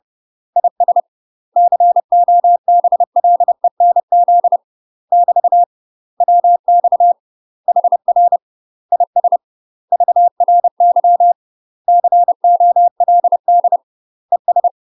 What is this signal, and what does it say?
Morse code, heard as a keyed tone, about 30 wpm, 700 Hz. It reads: IS COBLENZ <BT> WX HR IS VRY COLD ES
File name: UK MASTER QSOs TRAINING / 30WPM / UK MASTER QSOs 0011